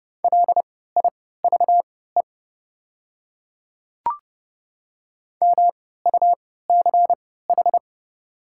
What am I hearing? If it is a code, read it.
Morse code, heard as a keyed tone, about 30 wpm, spaced as for 17 wpm, 700 Hz. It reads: LIVE MUCH
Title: 3x Character Spacing - Top 200 Words - 30wpm